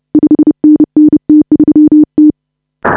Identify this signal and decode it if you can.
Morse code, heard as a keyed tone, about 33 wpm, 310 Hz. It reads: 5NN T3T
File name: AUDIO2019 / HA5BA CW